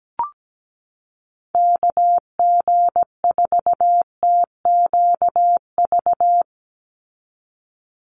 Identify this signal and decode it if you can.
Morse code, heard as a keyed tone, about 17 wpm, 700 Hz. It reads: KG4TQV
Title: USA Call Signs 17wpm